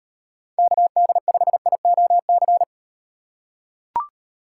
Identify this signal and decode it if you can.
Morse code, heard as a keyed tone, about 38 wpm, 700 Hz. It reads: KD5IOC